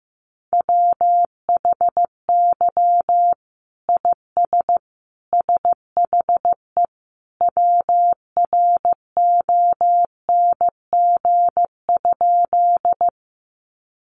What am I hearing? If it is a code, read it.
Morse code, heard as a keyed tone, about 15 wpm, 700 Hz. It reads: WHY IS SHE WRONG?
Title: Sentences from Top 900 Words 15wpm